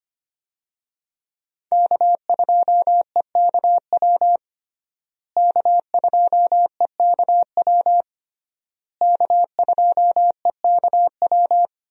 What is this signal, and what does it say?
Morse code, heard as a keyed tone, about 25 wpm, 700 Hz. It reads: K2EKW K2EKW K2EKW
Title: USA Call Signs - 5 Characters - Repeated 3x - 25wpm